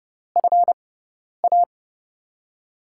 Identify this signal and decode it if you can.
Morse code, heard as a keyed tone, about 31 wpm, spaced as for 10 wpm, 700 Hz. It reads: FA
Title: USA Call Signs - 6 Characters - 6x Character Spacing - 30wpm